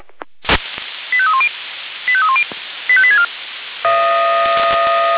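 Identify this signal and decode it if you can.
Five-tone selective call (ZVEI), heard as quick runs of five tones, tones of about 70 ms. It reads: 85310 85310 85864
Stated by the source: ZVEI five-tone call received by SDR.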